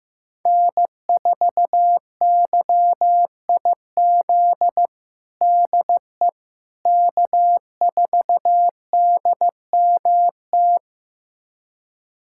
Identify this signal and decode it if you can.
Morse code, heard as a keyed tone, about 15 wpm, 700 Hz. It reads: N4YIZ DE K4DMT
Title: USA Callsign DE Callsign - 5 Characters - 15wpm